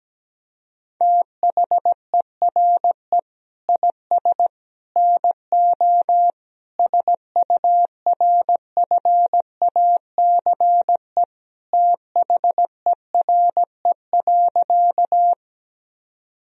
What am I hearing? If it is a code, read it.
Morse code, heard as a keyed tone, about 17 wpm, 700 Hz. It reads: THERE IS NO SURFACE THERE.
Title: Sentences from Top 1000 Words - Review Part 3 - 17wpm